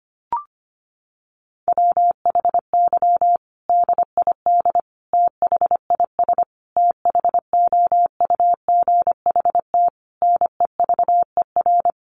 Morse code, heard as a keyed tone, about 25 wpm, 700 Hz. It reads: WHY DID THIS THOUGHT NEVER
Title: Sentences from Top 900 Words 25wpm